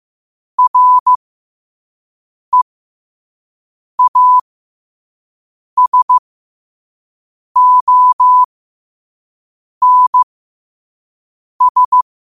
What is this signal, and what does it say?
Morse code, heard as a keyed tone, about 15 wpm, spaced as for 5 wpm, 1000 Hz. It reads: REASONS
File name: qotdfiles / 5wpm